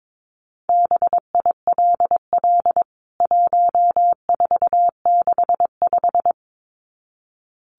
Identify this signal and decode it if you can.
Morse code, heard as a keyed tone, about 22 wpm, 700 Hz. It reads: BILL 1465
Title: CWT Exchange 22wpm